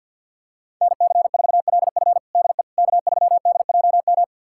Morse code, heard as a keyed tone, about 47 wpm, 700 Hz. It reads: NC4LF DE K3DJM
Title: USA Callsign DE Callsign - 5 Characters - 50wpm